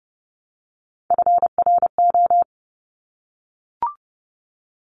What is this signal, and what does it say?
Morse code, heard as a keyed tone, about 30 wpm, 700 Hz. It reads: FRO
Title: Top Three Letter Combinations in Words 30wpm